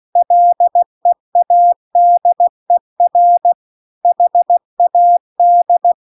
Morse code, heard as a keyed tone, about 16 wpm, 700 Hz. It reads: LEADER HAD
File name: CW-The-Big-Twoheaded-River-2-16WPM